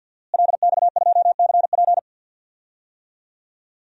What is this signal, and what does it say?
Morse code, heard as a keyed tone, about 46 wpm, 700 Hz. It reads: LX2CP
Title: DXCC-100 Call Signs - 5 Characters - Repeated 3x - 50wpm